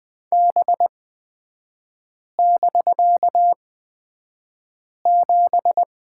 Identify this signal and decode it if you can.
Morse code, heard as a keyed tone, about 20 wpm, spaced as for 5 wpm, 700 Hz. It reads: B<BK>7
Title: 250 Lesson 40e - BK - Three Characters 20wpm